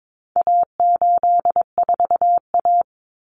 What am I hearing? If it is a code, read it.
Morse code, heard as a keyed tone, about 22 wpm, 700 Hz. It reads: A84A
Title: Common Cut Numbers - 4-Digit 22wpm